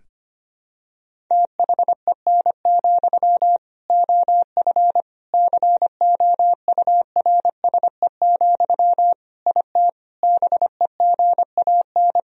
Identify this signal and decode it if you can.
Morse code, heard as a keyed tone, about 25 wpm, 700 Hz. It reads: THEN, OF COURSE, IT BEGAN
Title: Sentences from Top 1000 Words - Review Part 1 - 25wpm